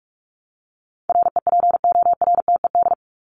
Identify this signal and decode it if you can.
Morse code, heard as a keyed tone, about 44 wpm, 700 Hz. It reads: REPORTED
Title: ED Suffix - Top 100 Words with Base Words 45wpm